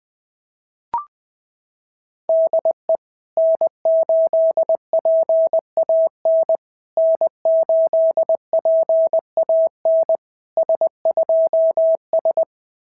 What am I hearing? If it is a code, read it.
Morse code, heard as a keyed tone, about 20 wpm, 660 Hz. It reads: DE N8PAN N8PAN S2S
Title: SOTA QSO Practice 20wpm